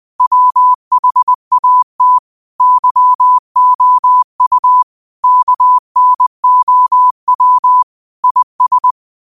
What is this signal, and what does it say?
Morse code, heard as a keyed tone, about 20 wpm, 1000 Hz. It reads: WHAT YOU KNOW IS
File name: qotdfiles / 20wpm